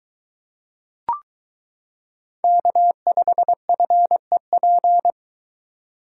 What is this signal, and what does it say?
Morse code, heard as a keyed tone, about 22 wpm, 700 Hz. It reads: K5FEP